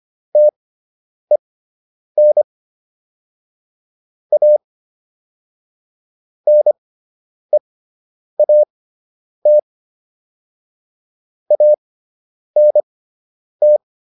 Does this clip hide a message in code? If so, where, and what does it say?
Yes: Morse code from 0.4 to 13.8 s, heard as a keyed tone, about 25 wpm, spaced as for 9 wpm, 600 Hz. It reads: TEN A NEAT ANT